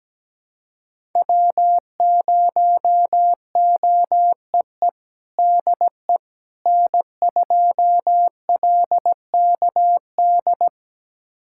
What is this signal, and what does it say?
Morse code, heard as a keyed tone, about 17 wpm, 700 Hz. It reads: W0OEE DE N2LKD
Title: USA Callsign DE Callsign - 5 Characters - 17wpm